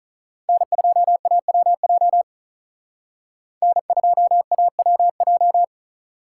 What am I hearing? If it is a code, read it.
Morse code, heard as a keyed tone, about 45 wpm, 700 Hz. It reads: N2AWJ N2AWJ